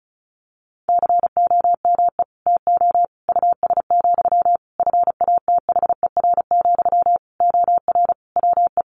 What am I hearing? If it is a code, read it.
Morse code, heard as a keyed tone, about 34 wpm, 700 Hz. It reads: COME TO US, FATHER, OR WE
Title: Sentences from Top 1300 Words 35wpm